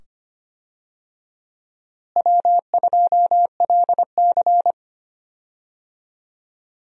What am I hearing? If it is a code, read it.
Morse code, heard as a keyed tone, about 25 wpm, 700 Hz. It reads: W2LC
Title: USA Call Signs - 4 Characters - Repeated 2x - 25wpm